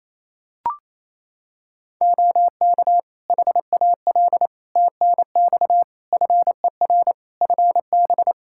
Morse code, heard as a keyed tone, about 28 wpm, 700 Hz. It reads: OK HAL TNX FER FB
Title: Practice QSO - Exchange 4 - Ending - 28wpm